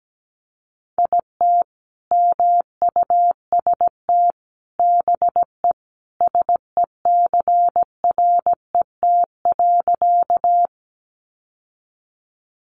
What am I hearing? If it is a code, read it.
Morse code, heard as a keyed tone, about 17 wpm, 700 Hz. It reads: IT MUST BE SECRET.
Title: Sentences from Top 1200 Words 17wpm